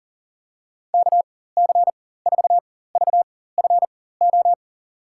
Morse code, heard as a keyed tone, about 40 wpm, spaced as for 19 wpm, 700 Hz. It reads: KC4VFO
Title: USA Call Signs - 6 Characters - 4x Character Spacing - 40wpm